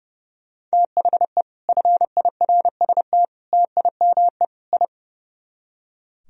Morse code, heard as a keyed tone, about 35 wpm, 700 Hz. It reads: THE FIRST TIME I
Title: Speed-Racing - Sets of 4 Words 30wpm